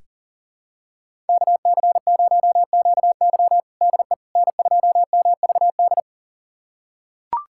Morse code, heard as a keyed tone, about 40 wpm, 700 Hz. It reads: KC0QY DE N2MVD
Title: USA Callsign DE Callsign - 5 Characters - 40wpm